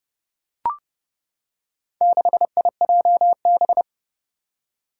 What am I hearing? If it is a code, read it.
Morse code, heard as a keyed tone, about 30 wpm, 700 Hz. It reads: 6IJB
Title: Four Challenging Letter-Number Combinations 30wpm